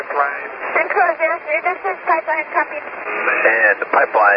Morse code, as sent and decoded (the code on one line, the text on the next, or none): none